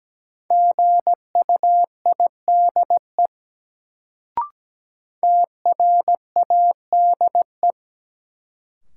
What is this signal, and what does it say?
Morse code, heard as a keyed tone, about 17 wpm, 700 Hz. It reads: GUIDE TRADE
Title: Sentences from Top 600 Words - New Word Drill - 17wpm